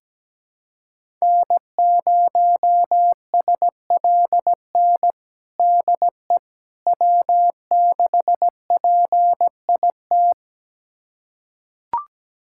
Morse code, heard as a keyed tone, about 17 wpm, 700 Hz. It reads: N0SLN DE W6PIT
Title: USA Callsign DE Callsign - 5 Characters - 17wpm